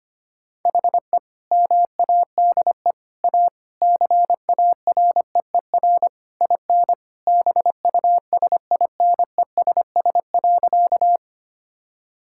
Morse code, heard as a keyed tone, about 25 wpm, 700 Hz. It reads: HE MADE A CAREER IN BUSINESS.
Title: Sentences from Top 1000 Words - Review Part 6 - 25wpm